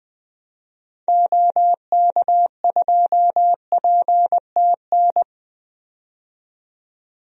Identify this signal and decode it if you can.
Morse code, heard as a keyed tone, about 20 wpm, 700 Hz. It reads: OK2PTN
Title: DXCC-100 Call Signs - 6 Characters - 20wpm